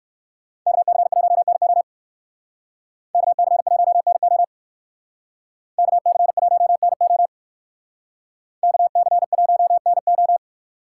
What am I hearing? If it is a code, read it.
Morse code, heard as a keyed tone, about 57 wpm, 700 Hz. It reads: KC1NO KC1NO KC1NO KC1NO